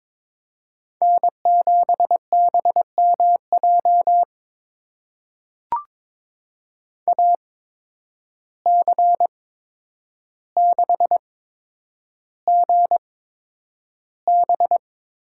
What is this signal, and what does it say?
Morse code, heard as a keyed tone, about 22 wpm, spaced as for 9 wpm, 700 Hz. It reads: N7BMJ AC6GB
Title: USA Call Signs - 5 Characters - 8x Character Spacing - 22wpm